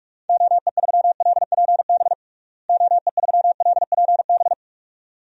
Morse code, heard as a keyed tone, about 46 wpm, 700 Hz. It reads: OE3LPB OE3LPB
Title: DXCC-100 Call Signs - 6 Characters - Repeated 2x - 45wpm